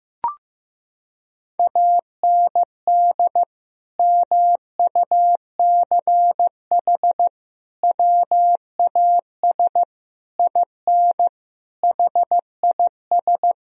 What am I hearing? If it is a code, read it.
Morse code, heard as a keyed tone, about 16 wpm, 700 Hz. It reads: AND MUCH WAS IN HIS
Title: Sentences from Top 1000 Words - Review Part 2 - 15wpm